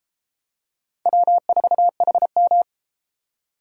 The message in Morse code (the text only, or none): W4HM